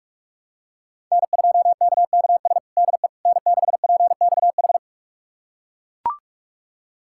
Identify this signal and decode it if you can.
Morse code, heard as a keyed tone, about 41 wpm, 700 Hz. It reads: N2KKS DE N6PXH
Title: USA Callsign DE Callsign - 5 Characters - 45wpm